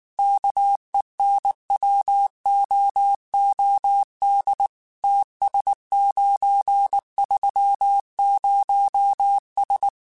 Morse code, heard as a keyed tone, about 19 wpm, 800 Hz. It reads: KENWOOD TS930S
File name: QSOs / QSOs-mp3s / kc7sac@18